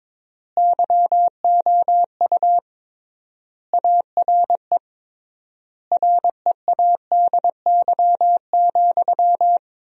Morse code, heard as a keyed tone, about 22 wpm, 700 Hz. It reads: YOU ARE READY,
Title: 3x Word-Spacing - Sentences from Top 500 Words - 22wpm